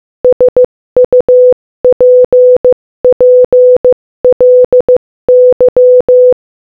Morse code, heard as a keyed tone, about 15 wpm, 500 Hz. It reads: SUPPLY
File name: Morse-Code-Audio-Speed-15-6